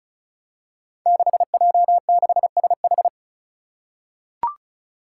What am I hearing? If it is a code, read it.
Morse code, heard as a keyed tone, about 36 wpm, 700 Hz. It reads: 6J6SH
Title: Five Challenging Letter-Number Combinations 35wpm